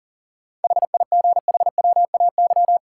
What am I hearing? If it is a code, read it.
Morse code, heard as a keyed tone, about 40 wpm, 700 Hz. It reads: HIGHWAY
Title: Sentences from Top 1400 Words - New Word Drill Alt - 40wpm